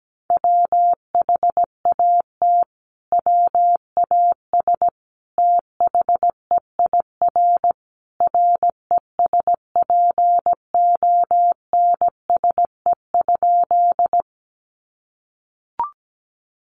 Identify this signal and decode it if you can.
Morse code, heard as a keyed tone, about 17 wpm, 700 Hz. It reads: WHAT WAS THEIR RESPONSE?
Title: Sentences from Top 1400 Words 17wpm